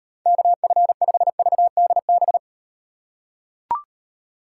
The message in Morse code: KF5VDB

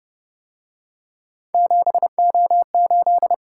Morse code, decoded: 7O8

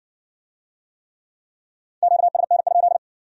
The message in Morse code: QSN?

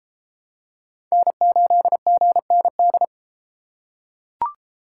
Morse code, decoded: N8GND

